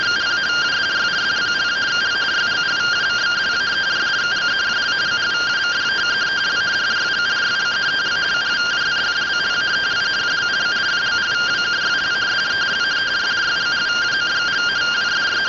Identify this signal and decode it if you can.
Two-tone teleprinter signal, about 170 Hz shift.